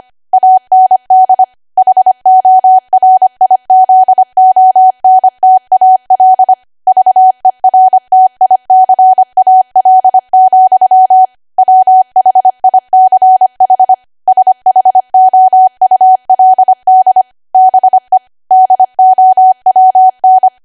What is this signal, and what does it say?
Morse code, heard as a keyed tone, about 25 wpm, 750 Hz. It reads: AND HORIZONTAL VERTICAL, WHICH SHOULD BE DOWN
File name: Archive / 25 WPM / 25WPM